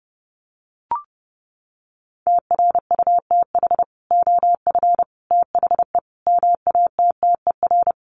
Morse code, heard as a keyed tone, about 30 wpm, 700 Hz. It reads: TRUTH OF THE MATTER
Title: Sets of 5 Words Encore2 30wpm